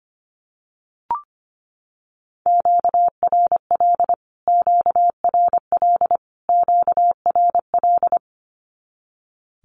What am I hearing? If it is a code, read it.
Morse code, heard as a keyed tone, about 25 wpm, 700 Hz. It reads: QRL QRL QRL
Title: Q-Codes Repeated 3x 25wpm